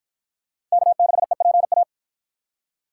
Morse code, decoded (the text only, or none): K6EPA